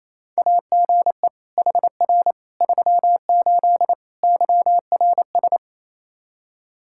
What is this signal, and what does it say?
Morse code, heard as a keyed tone, about 27 wpm, 700 Hz. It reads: AGE HR 38 YRS
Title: Practice QSO - Exchange 3 - More Info - 28wpm